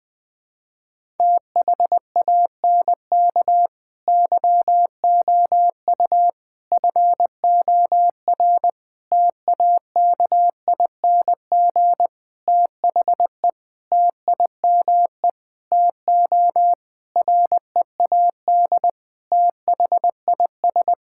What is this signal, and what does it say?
Morse code, heard as a keyed tone, about 20 wpm, 700 Hz. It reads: THANK YOU FOR TAKING THE TIME TO READ THIS
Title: Sentences with 9-Word Phrases - 20wpm